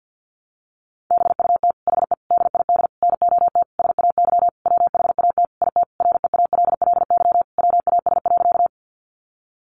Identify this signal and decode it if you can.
Morse code, heard as a keyed tone, about 50 wpm, 700 Hz. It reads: BUT HE DID NOT SAY WHAT IT REALLY WAS.
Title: Sentences from Top 600 Words 50wpm